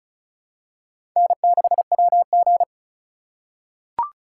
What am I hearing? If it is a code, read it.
Morse code, heard as a keyed tone, about 36 wpm, 700 Hz. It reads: N6WG
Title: USA Call Signs - 4 Characters - 3x Character Spacing - 35wpm